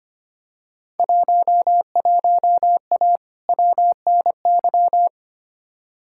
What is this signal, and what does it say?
Morse code, heard as a keyed tone, about 25 wpm, 700 Hz. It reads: 11A WNY
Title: Field Day Exchanges Encore - 25wpm